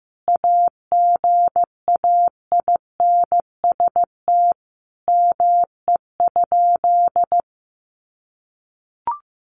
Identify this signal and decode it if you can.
Morse code, heard as a keyed tone, about 15 wpm, 700 Hz. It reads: AGAINST ME?
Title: Sentences from Top 1000 Words 15wpm